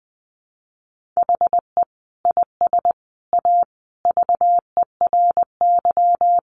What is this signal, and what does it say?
Morse code, heard as a keyed tone, about 20 wpm, 700 Hz. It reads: HE IS A VERY
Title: Sentences from Top 1200 Words 20wpm